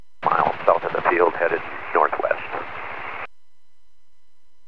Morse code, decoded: none

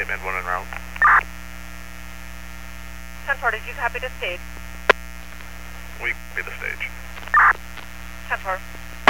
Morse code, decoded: none

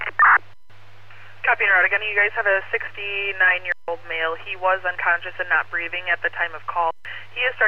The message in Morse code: none

